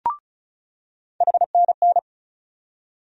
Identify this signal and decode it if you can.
Morse code, heard as a keyed tone, about 36 wpm, 700 Hz. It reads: HNN
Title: USA Call Sign Postfixes - 3 Letters 35wpm